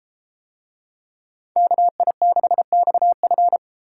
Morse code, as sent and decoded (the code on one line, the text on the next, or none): -.- .. -.... -..- ..-.
KI6XF